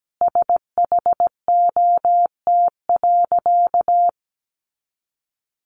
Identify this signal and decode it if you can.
Morse code, heard as a keyed tone, about 17 wpm, 700 Hz. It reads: SHOT.
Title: A Sound Of Thunder 17wpm